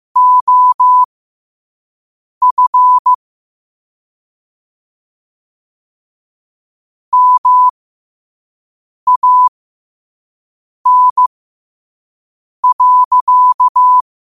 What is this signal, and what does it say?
Morse code, heard as a keyed tone, about 15 wpm, spaced as for 5 wpm, 1000 Hz. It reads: OF MAN.
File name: qotdfiles / 5wpm